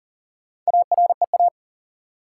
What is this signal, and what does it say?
Morse code, heard as a keyed tone, about 40 wpm, 700 Hz. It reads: AREA
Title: Sentences from Top 1000 Words - Review Part 2 - New Word Drill - 40wpm